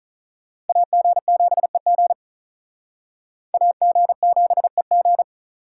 Morse code, heard as a keyed tone, about 46 wpm, 700 Hz. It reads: AG7EG AG7EG